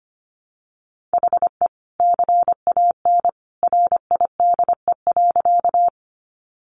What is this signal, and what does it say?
Morse code, heard as a keyed tone, about 25 wpm, 700 Hz. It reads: HE CAN RIDE.
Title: Sentences from Top 1200 Words 25wpm